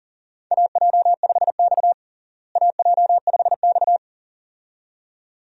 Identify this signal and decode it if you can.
Morse code, heard as a keyed tone, about 40 wpm, 700 Hz. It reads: AJ5X AJ5X
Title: USA Call Signs - 4 Characters - Repeated 2x - 40wpm